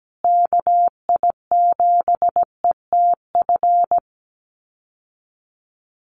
Morse code, heard as a keyed tone, about 17 wpm, 700 Hz. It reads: KI7ETF
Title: USA Call Signs Repeated 3x 17wpm